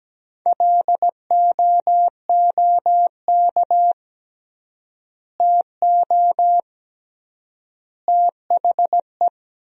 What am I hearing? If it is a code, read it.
Morse code, heard as a keyed tone, about 17 wpm, 700 Hz. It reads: LOOK TO THE
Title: 3x Word-Spacing - Sentences from Top 400 Words - 17wpm